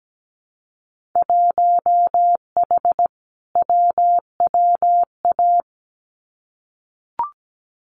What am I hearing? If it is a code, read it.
Morse code, heard as a keyed tone, about 17 wpm, 700 Hz. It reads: 1H WWA
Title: Winter Field Day Exchanges 17wpm